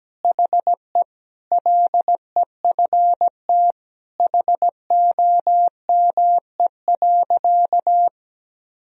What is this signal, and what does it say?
Morse code, heard as a keyed tone, about 17 wpm, 700 Hz. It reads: HE LEFT HOME.
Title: Sentences from Top 1000 Words - Review Part 4 - 17wpm